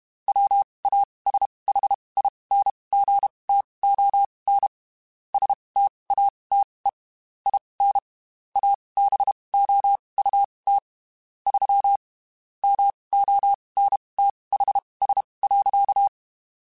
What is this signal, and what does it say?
Morse code, heard as a keyed tone, about 32 wpm, spaced as for 23 wpm, 800 Hz. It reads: WASHINGTON STATE IN ABOUT 3 MONTHS.
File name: QSOs / QSOs-mp3s / WD0FBT 1@30